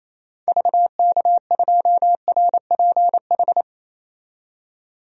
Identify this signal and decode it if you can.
Morse code, heard as a keyed tone, about 28 wpm, 700 Hz. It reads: VK2RPH
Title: DXCC-100 Call Signs - 6 Characters - 28wpm